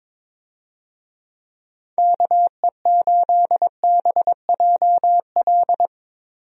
Morse code, heard as a keyed tone, about 22 wpm, 700 Hz. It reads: KE8BJL
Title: USA Call Signs - 6 Characters - Repeated 2x - 22wpm